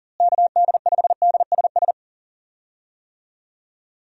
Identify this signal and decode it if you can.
Morse code, heard as a keyed tone, about 39 wpm, 700 Hz. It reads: KD5DSS
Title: USA Call Signs - 6 Characters - Repeated 2x - 40wpm